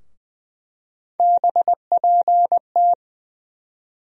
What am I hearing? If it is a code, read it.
Morse code, heard as a keyed tone, about 20 wpm, 700 Hz. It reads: BPT
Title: USA Call Sign Postfixes - 3 Letters 20wpm